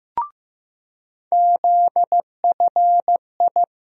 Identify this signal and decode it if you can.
Morse code, heard as a keyed tone, about 15 wpm, 700 Hz. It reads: ZFI
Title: USA Call Sign Postfixes - 3 Letters 15wpm